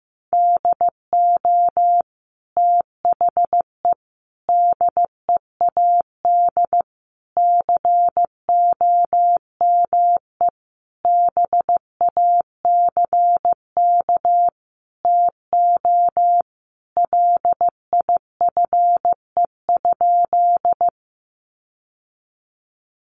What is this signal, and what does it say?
Morse code, heard as a keyed tone, about 15 wpm, 700 Hz. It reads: DO THE DEAD COME BACK TO LIFE?
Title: Sentences from Top 1500 Words 15wpm